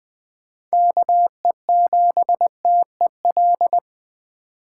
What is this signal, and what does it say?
Morse code, heard as a keyed tone, about 20 wpm, 700 Hz. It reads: KE7TEL
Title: USA Call Signs - 8x Character Spacing - 20wpm